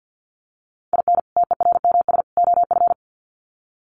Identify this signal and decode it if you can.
Morse code, heard as a keyed tone, about 48 wpm, 700 Hz. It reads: IN TERMS OF
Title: Sets of 3 Words 50wpm